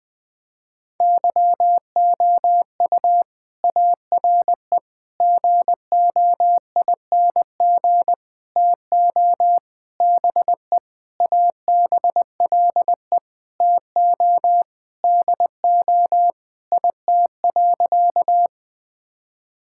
Morse code, heard as a keyed tone, about 20 wpm, 700 Hz. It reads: YOU ARE GOING TO BE ABLE TO DO IT.
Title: Sentences with 4-Word Phrases - 20wpm